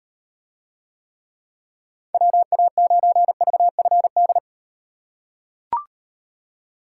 Morse code, heard as a keyed tone, about 38 wpm, 700 Hz. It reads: WA9VFD